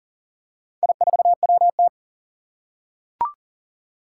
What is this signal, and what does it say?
Morse code, heard as a keyed tone, about 40 wpm, 700 Hz. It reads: I4WT
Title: Four Challenging Letter-Number Combinations 40wpm